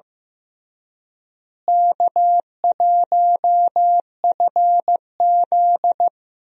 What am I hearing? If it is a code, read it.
Morse code, heard as a keyed tone, about 15 wpm, 700 Hz. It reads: K1FZ
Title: USA Call Signs - 4 Characters - Repeated 3x - 15wpm